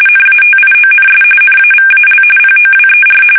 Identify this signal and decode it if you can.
RTTY, about 45.45 baud. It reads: NATIONS NAVAL POWER
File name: fsk850 / usnavy5